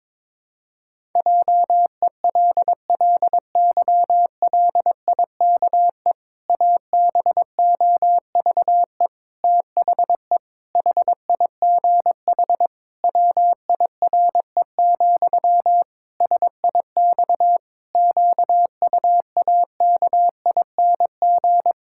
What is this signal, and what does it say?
Morse code, heard as a keyed tone, about 21 wpm, 700 Hz. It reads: JELLYLIKE ABOVE THE HIGH WIRE, SIX QUAKING
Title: Pangram Challenge v2 - 22wpm